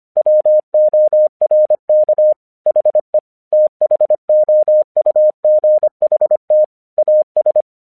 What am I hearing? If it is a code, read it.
Morse code, heard as a keyed tone, about 25 wpm, 610 Hz. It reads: WORK HE THOUGHT AS
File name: 1 English 25wpm / AQC30